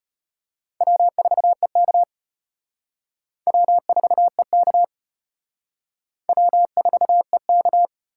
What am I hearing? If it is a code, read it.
Morse code, heard as a keyed tone, about 33 wpm, 700 Hz. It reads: W4EK W4EK W4EK